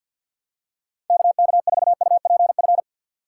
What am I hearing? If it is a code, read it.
Morse code, heard as a keyed tone, about 46 wpm, 700 Hz. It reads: KK4UPF
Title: USA Call Signs - 6 Characters - 3x Character Spacing - 50wpm